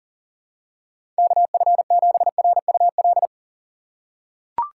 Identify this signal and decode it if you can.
Morse code, heard as a keyed tone, about 40 wpm, 700 Hz. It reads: KF7RUL
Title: USA Call Signs - 6 Characters - 6x Character Spacing - 40wpm